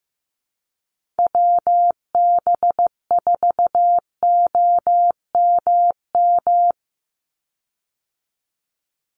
Morse code, heard as a keyed tone, about 15 wpm, 700 Hz. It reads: WB4OMM
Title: Call Signs Repeated 2x - 15wpm